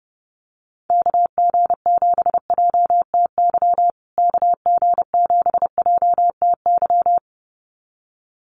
Morse code, heard as a keyed tone, about 30 wpm, 700 Hz. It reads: KG7JTY KG7JTY
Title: USA Call Signs Repeated 2x 30wpm